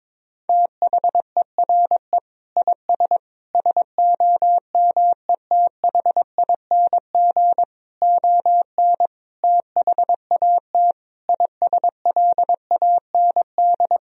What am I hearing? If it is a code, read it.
Morse code, heard as a keyed tone, about 22 wpm, 690 Hz. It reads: THERE IS SOMETHING ON THAT ISLAND
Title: Sentences from Top 1000 Words - Review Part 3 - 22wpm